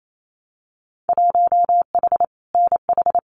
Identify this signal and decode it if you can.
Morse code, heard as a keyed tone, about 28 wpm, 700 Hz. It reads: 1H NH
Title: Winter Field Day Exchanges 28wpm